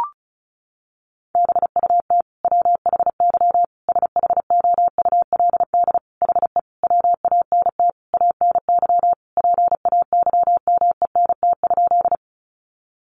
Morse code, heard as a keyed tone, about 34 wpm, 700 Hz. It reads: BUT WHY SHOULD HE WANT ANY PAYMENT?
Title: Sentences from Top 1200 Words 35wpm